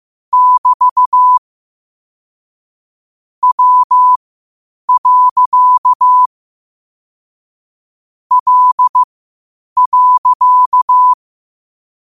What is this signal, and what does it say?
Morse code, heard as a keyed tone, about 15 wpm, spaced as for 8 wpm, 1000 Hz. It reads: <BT> W. L.
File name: qotdfiles / 8wpm